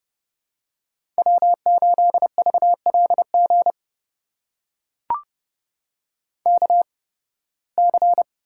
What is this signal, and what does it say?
Morse code, heard as a keyed tone, about 29 wpm, spaced as for 19 wpm, 700 Hz. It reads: W8VLG KC
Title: USA Call Signs - 8x Character Spacing - 30wpm